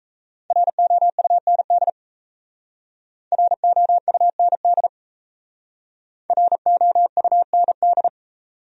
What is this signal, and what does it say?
Morse code, heard as a keyed tone, about 37 wpm, 700 Hz. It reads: ROUND ROUND ROUND